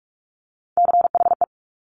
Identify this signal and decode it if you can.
Morse code, heard as a keyed tone, about 46 wpm, 700 Hz. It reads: CHE
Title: Top Three Letter Combinations in Words 45wpm